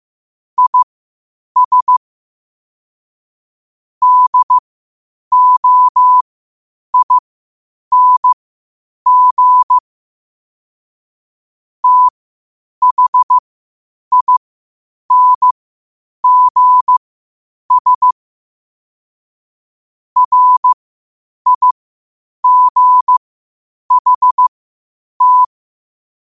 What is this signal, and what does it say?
Morse code, heard as a keyed tone, about 15 wpm, spaced as for 8 wpm, 1000 Hz. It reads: IS DOING THINGS RIGHT